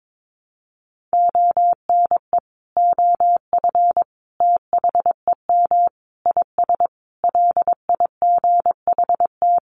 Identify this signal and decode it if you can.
Morse code, heard as a keyed tone, about 22 wpm, 700 Hz. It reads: ONE OF THEM IS LIGHT
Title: Additional Easy Phrases 22wpm